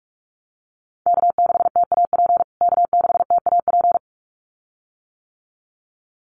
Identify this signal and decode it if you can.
Morse code, heard as a keyed tone, about 44 wpm, 700 Hz. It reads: K6TAP K6TAP
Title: USA Call Signs Repeated 2x 45wpm